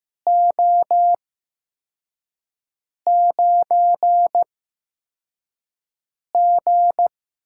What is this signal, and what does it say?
Morse code, heard as a keyed tone, about 15 wpm, spaced as for 4 wpm, 700 Hz. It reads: O9G
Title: USA Call Signs - 4 Characters - 8x Character Spacing - 15wpm